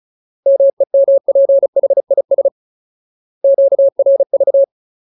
Morse code, heard as a keyed tone, about 34 wpm, 550 Hz. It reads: MEMPHIS QRV
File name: Ultimate Word List / 35WPM Ultimate